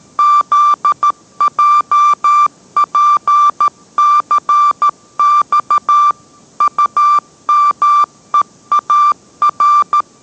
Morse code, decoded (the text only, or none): ZJPCX UMEAR